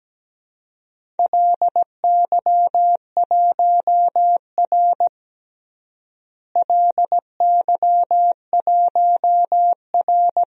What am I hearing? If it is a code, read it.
Morse code, heard as a keyed tone, about 17 wpm, 700 Hz. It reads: LY1R LY1R
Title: DXCC-100 Call Signs - 4 Characters - Repeated 2x - 17wpm